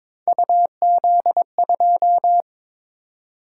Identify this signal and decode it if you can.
Morse code, heard as a keyed tone, about 22 wpm, 700 Hz. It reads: UZ2
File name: practice / 22wpm / Three Letter-Number Combinations - 22wpm